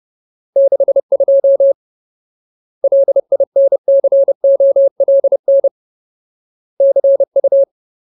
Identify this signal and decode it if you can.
Morse code, heard as a keyed tone, about 30 wpm, 550 Hz. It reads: 62 LINCOLN CU
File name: Ultimate Word List / 30WPM Ultimate